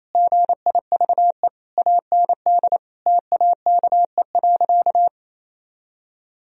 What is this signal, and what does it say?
Morse code, heard as a keyed tone, about 28 wpm, 700 Hz. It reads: GIVE AND TAKE.
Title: Sentences from Top 1000 Words - Review Part 4 - 28wpm